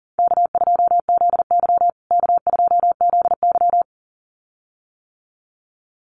Morse code, heard as a keyed tone, about 40 wpm, 700 Hz. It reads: K2ZY K2ZY
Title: USA Call Signs Repeated 2x 40wpm